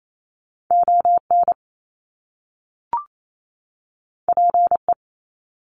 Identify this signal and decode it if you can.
Morse code, heard as a keyed tone, about 28 wpm, 700 Hz. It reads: ON PE
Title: Canadian Provinces and Territories 28wpm